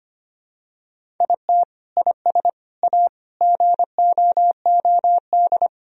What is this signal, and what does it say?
Morse code, heard as a keyed tone, about 25 wpm, 700 Hz. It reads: IT IS A GOOD
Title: Sentences from Top 1000 Words - Review Part 1 - 25wpm